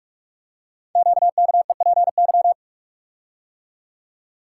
Morse code, heard as a keyed tone, about 44 wpm, 700 Hz. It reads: QKEPY
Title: Code Groups - 5 Letters - 45wpm